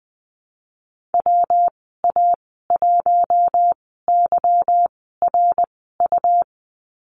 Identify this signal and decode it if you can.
Morse code, heard as a keyed tone, about 20 wpm, spaced as for 15 wpm, 700 Hz. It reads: WA1YRU
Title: 161 Lesson 26g - Letter g - Call Signs 20wpm